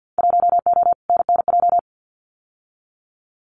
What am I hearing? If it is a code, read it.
Morse code, heard as a keyed tone, about 50 wpm, 700 Hz. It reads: 1O NNJ
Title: Winter Field Day Exchanges 50wpm